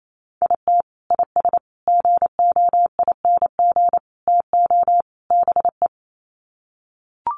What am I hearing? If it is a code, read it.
Morse code, heard as a keyed tone, about 28 wpm, 700 Hz. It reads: IT IS GOING TO BE
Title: Sets of 5 Words 28wpm